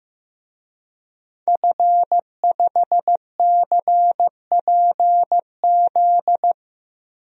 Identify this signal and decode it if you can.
Morse code, heard as a keyed tone, about 15 wpm, 700 Hz. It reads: F5CPZ
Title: DXCC-100 Call Signs - 5 Characters - 15wpm